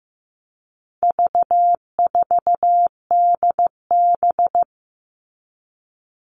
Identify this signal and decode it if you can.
Morse code, heard as a keyed tone, about 15 wpm, 700 Hz. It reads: V4DB
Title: All Cut Numbers - 4-Digit 15wpm